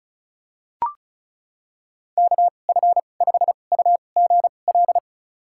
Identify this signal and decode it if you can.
Morse code, heard as a keyed tone, about 36 wpm, spaced as for 25 wpm, 700 Hz. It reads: KF5UGL
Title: USA Call Signs - 6 Characters - 2x Character Spacing - 35wpm